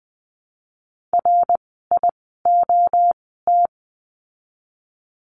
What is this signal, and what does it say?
Morse code, heard as a keyed tone, about 20 wpm, spaced as for 14 wpm, 700 Hz. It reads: RIOT
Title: 054 Lesson 10f - Letter r - Words 20wpm